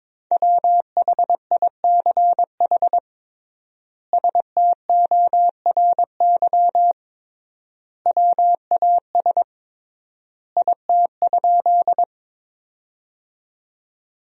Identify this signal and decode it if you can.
Morse code, heard as a keyed tone, about 22 wpm, 700 Hz. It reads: WHICH STORY WAS IT?
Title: 3x Word-Spacing - Sentences from Top 300 Words - 22wpm